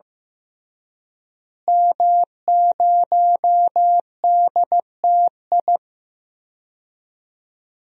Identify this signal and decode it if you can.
Morse code, heard as a keyed tone, about 15 wpm, 700 Hz. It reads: M0DTI
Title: DXCC-100 Call Signs - 5 Characters - Repeated 3x - 15wpm